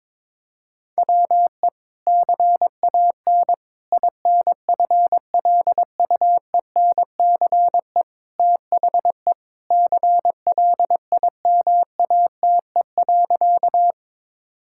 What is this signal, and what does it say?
Morse code, heard as a keyed tone, about 22 wpm, 700 Hz. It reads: WE CAN INFLUENCE THE CLIMATE.
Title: Sentences from Top 1600 Words 22wpm